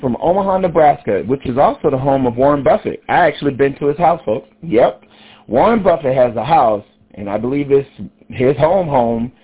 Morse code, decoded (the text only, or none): none